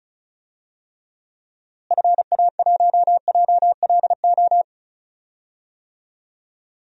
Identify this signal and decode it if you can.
Morse code, heard as a keyed tone, about 36 wpm, 700 Hz. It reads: FA1JLO
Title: DXCC-100 Call Signs - 6 Characters - Repeated 2x - 35wpm